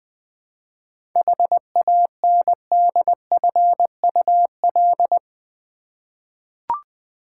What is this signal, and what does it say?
Morse code, heard as a keyed tone, about 20 wpm, 700 Hz. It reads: HANDFUL
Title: 7 Letter Words - First 3 Letters Given - 20wpm